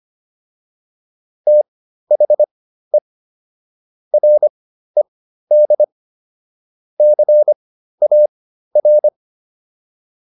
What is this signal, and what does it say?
Morse code, heard as a keyed tone, about 25 wpm, spaced as for 13 wpm, 600 Hz. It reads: THE RED CAR